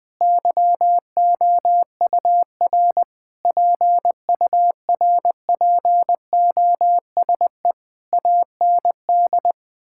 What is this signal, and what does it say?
Morse code, heard as a keyed tone, about 20 wpm, 700 Hz. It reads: YOUR PURPOSE AND
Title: Sentences with 9-Word Phrases - Spoken Beginning - 20wpm